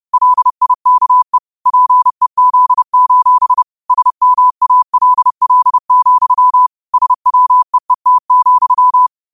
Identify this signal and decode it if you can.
Morse code, heard as a keyed tone, about 30 wpm, 1000 Hz. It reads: LIKE PEZ: SMALL, SWEET,